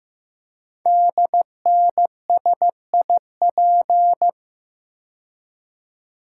Morse code, heard as a keyed tone, about 15 wpm, 700 Hz. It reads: DNSIP
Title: Code Groups - 5 Letters - 15wpm